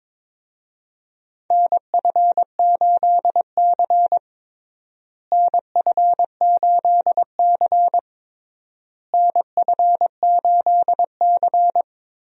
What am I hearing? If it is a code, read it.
Morse code, heard as a keyed tone, about 22 wpm, 700 Hz. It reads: NF8C NF8C NF8C
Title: USA Call Signs - 4 Characters - Repeated 3x - 22wpm